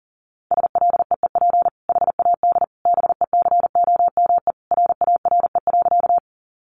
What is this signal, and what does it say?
Morse code, heard as a keyed tone, about 40 wpm, 700 Hz. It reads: SLEEP HAD BECOME RARE.
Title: Sentences from Top 1100 Words 40wpm